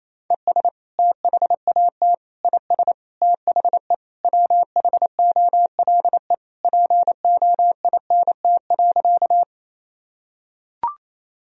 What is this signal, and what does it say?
Morse code, heard as a keyed tone, about 28 wpm, 700 Hz. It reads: ES THAT IS THE WHOLE POINT.
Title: Sentences from Top 400 Words and Abbreviations 28wpm